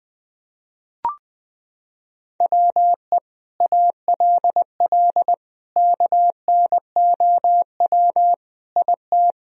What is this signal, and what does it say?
Morse code, heard as a keyed tone, about 20 wpm, 700 Hz. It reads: WE ALL KNOW IT
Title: Sentences from Top 1000 Words - Review Part 1 - 20wpm